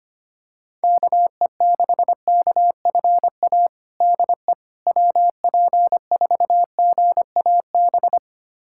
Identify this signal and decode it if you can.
Morse code, heard as a keyed tone, about 25 wpm, 700 Hz. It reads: KE6KFA DE WP4GAB
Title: USA Callsign DE Callsign - 6 Characters - 25wpm